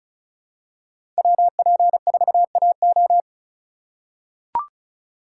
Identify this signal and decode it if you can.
Morse code, heard as a keyed tone, about 36 wpm, 700 Hz. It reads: WP4AO
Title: USA Call Signs - 5 Characters - 4x Character Spacing - 35wpm